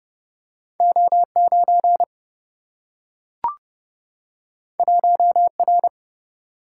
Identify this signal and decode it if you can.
Morse code, heard as a keyed tone, about 30 wpm, 700 Hz. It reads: O 9 1 R
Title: Single-Letter Single-Number 30wpm